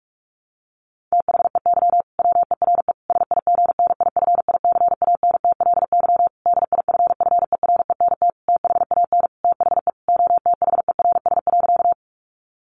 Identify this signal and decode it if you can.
Morse code, heard as a keyed tone, about 44 wpm, 700 Hz. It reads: THEY WERE SIGNIFICANTLY DIFFERENT THAN THE OTHERS.
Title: Sentences from Top 1500 Words 45wpm